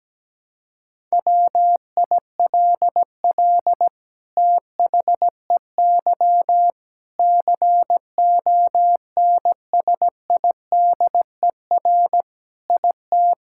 Morse code, heard as a keyed tone, about 17 wpm, 700 Hz. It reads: WILL THEY CONSIDER IT
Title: Sentences from Top 1000 Words - Review Part 5 - 17wpm